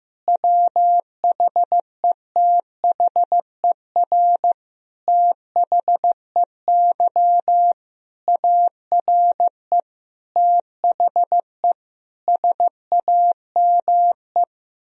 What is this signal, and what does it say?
Morse code, heard as a keyed tone, about 15 wpm, 700 Hz. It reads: WHETHER THEY ARE THE SAME
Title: Sentences from Top 1000 Words - Review Part 4 - 15wpm